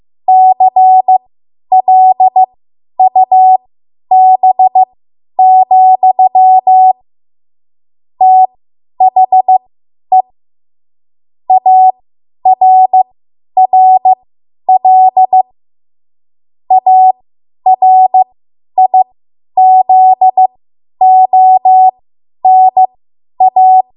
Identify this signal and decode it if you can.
Morse code, heard as a keyed tone, about 15 wpm, spaced as for 10 wpm, 750 Hz. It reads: CLUB, THE ARRL ARIZONA
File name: Morse / 10WPM